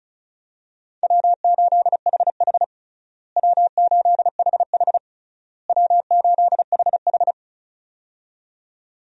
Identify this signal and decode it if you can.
Morse code, heard as a keyed tone, about 36 wpm, 700 Hz. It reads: W8HH W8HH W8HH
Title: USA Call Signs - 4 Characters - Repeated 3x - 35wpm